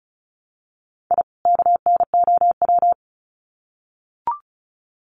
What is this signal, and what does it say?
Morse code, heard as a keyed tone, about 34 wpm, 700 Hz. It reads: I KNOW
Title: Sets of 2 Words 35wpm